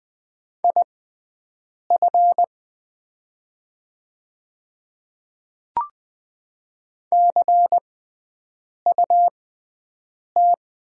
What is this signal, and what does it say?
Morse code, heard as a keyed tone, about 20 wpm, spaced as for 7 wpm, 700 Hz. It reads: IF CUT
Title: 6x Character Spacing - Top 300 Words - 20wpm